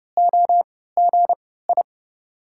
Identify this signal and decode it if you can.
Morse code, heard as a keyed tone, about 32 wpm, spaced as for 17 wpm, 700 Hz. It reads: OGI
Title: USA Call Signs - 6 Characters - 3x Character Spacing - 30wpm